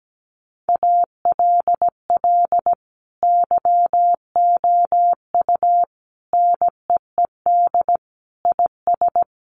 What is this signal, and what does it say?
Morse code, heard as a keyed tone, about 17 wpm, 700 Hz. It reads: ALL YOU NEED IS
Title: Sentences from Top 1500 Words 17wpm